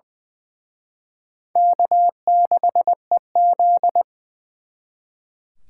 Morse code, heard as a keyed tone, about 24 wpm, 700 Hz. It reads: K6EZ
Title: Speed Racing - USA 1x2 Call Signs 20wpm